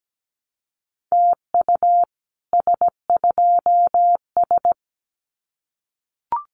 Morse code, heard as a keyed tone, about 17 wpm, 700 Hz. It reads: TU S2S
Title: Common QSO Phrases Encore 17wpm